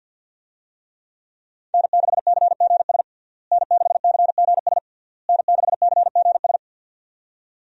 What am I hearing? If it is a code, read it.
Morse code, heard as a keyed tone, about 47 wpm, 700 Hz. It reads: N6CGS N6CGS N6CGS
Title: USA Call Signs - 5 Characters - Repeated 3x - 50wpm